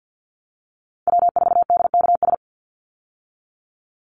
Morse code, heard as a keyed tone, about 50 wpm, 700 Hz. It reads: W4DKS
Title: Call Signs - 50wpm